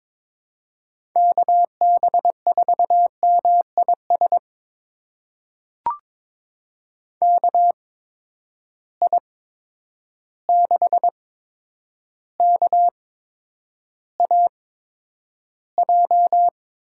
Morse code, heard as a keyed tone, about 22 wpm, spaced as for 9 wpm, 700 Hz. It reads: KB4MIS KI6KAJ
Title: USA Call Signs - 6 Characters - 8x Character Spacing - 22wpm